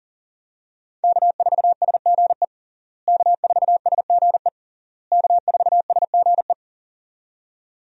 Morse code, heard as a keyed tone, about 37 wpm, 700 Hz. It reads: K4SGE K4SGE K4SGE
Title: USA Call Signs - 5 Characters - Repeated 3x - 40wpm